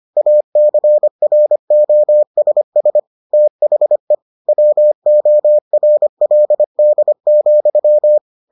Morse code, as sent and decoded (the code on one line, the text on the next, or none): .- -.-. .-. --- ... ... / - .... / .-- --- .-. .-.. -.. --..--
ACROSS THE WORLD,